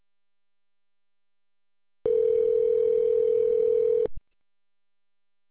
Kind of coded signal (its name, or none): telephone call progress tone